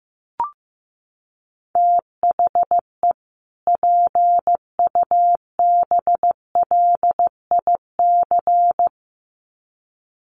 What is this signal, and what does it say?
Morse code, heard as a keyed tone, about 15 wpm, 700 Hz. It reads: THE PUBLIC
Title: Sets of 2 Words 15wpm